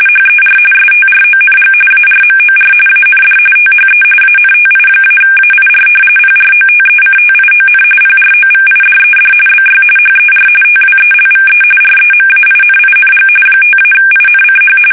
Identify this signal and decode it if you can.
RTTY, about 45.45 baud. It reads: HER STRONG ARMOR AND THE ANTI-AIR PROTECTION PROVIDED BY HER TWELVE REMAINING 5-INCH
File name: fsk850 / battleshipnj